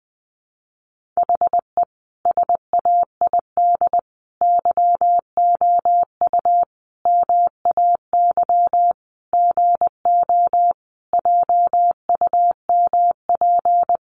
Morse code, heard as a keyed tone, about 20 wpm, 700 Hz. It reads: HE SAID YOU MAY GO JUMP
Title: Sentences from Top 1300 Words 20wpm